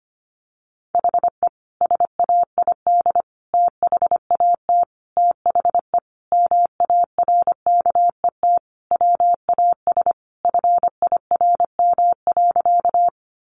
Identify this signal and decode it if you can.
Morse code, heard as a keyed tone, about 25 wpm, 700 Hz. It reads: HE SAID THAT THE MARKET WAS FIRM.
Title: Sentences from Top 1400 Words 25wpm